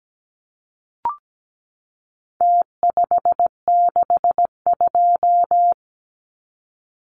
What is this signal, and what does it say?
Morse code, heard as a keyed tone, about 17 wpm, 700 Hz. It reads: T562
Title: Numbers - Contest Style - 4-Digit 17wpm